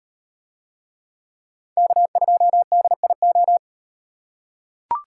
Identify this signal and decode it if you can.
Morse code, heard as a keyed tone, about 35 wpm, 700 Hz. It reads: K2DIO